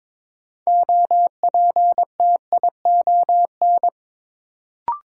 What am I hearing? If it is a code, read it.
Morse code, heard as a keyed tone, about 22 wpm, 700 Hz. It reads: OPTION
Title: Sentences from Top 1300 Words - New Word Drill Alt - 22wpm